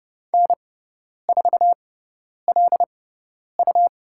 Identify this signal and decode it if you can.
Morse code, heard as a keyed tone, about 30 wpm, spaced as for 10 wpm, 700 Hz. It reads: N4LU
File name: Lesson 18g - Letter m - Call Signs 30wpm